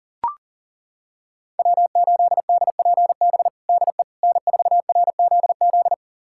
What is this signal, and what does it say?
Morse code, heard as a keyed tone, about 37 wpm, 700 Hz. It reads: W8DPB DE N4RZZ
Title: USA Callsign DE Callsign - 5 Characters - 40wpm